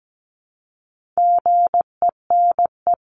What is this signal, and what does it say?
Morse code, heard as a keyed tone, about 17 wpm, 700 Hz. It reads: GENE
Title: CWT Exchange 17wpm